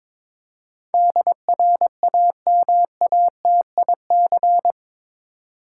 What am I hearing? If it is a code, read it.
Morse code, heard as a keyed tone, about 21 wpm, 700 Hz. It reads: DRAMATIC
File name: practice / 22wpm / Eight Letter Words - 22wpm